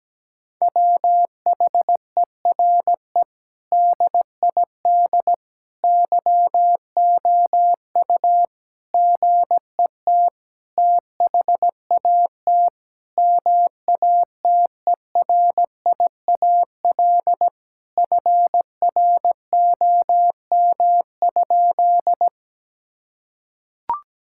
Morse code, heard as a keyed tone, about 17 wpm, 700 Hz. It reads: WHERE DID YOU GET THAT MATERIAL FROM?
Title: Sentences from Top 1000 Words - Review Part 5 - 17wpm